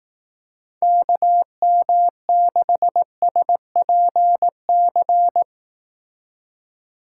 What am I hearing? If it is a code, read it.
Morse code, heard as a keyed tone, about 19 wpm, 700 Hz. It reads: KM6SPC
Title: Speed Racing - USA Call Signs 15wpm